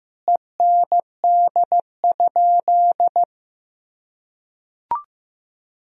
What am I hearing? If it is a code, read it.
Morse code, heard as a keyed tone, about 15 wpm, 700 Hz. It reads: END?
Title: Sentences from Top 1000 Words - Review Part 4 - 15wpm